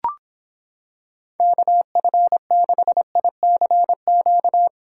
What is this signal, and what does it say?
Morse code, heard as a keyed tone, about 25 wpm, 700 Hz. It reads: KF6ICQ